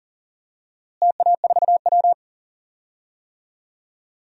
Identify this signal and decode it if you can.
Morse code, heard as a keyed tone, about 40 wpm, 700 Hz. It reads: TA4W
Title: DXCC-100 Call Signs - 4 Characters - Repeated 3x - 40wpm